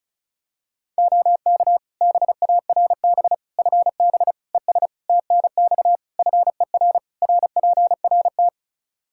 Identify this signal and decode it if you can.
Morse code, heard as a keyed tone, about 36 wpm, 700 Hz. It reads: OK BARB FB ES TNX FER RPRT
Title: Practice QSO - Exchange 2 - Info - 35wpm